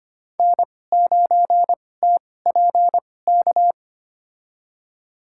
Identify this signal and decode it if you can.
Morse code, heard as a keyed tone, about 25 wpm, spaced as for 18 wpm, 700 Hz. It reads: N9TPK
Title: USA Call Signs - 5 Characters - 2x Character Spacing - 25wpm